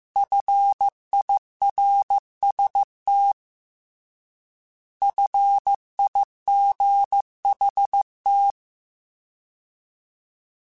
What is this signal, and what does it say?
Morse code, heard as a keyed tone, about 15 wpm, 800 Hz. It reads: FIRST FIGHT
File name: Alphabetical / Alphabetical mp3s / Fs @13